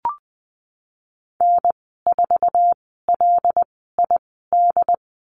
Morse code, heard as a keyed tone, about 20 wpm, spaced as for 15 wpm, 700 Hz. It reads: N4LID
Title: 088 Lesson 15g - Number 5 - Call Signs 20wpm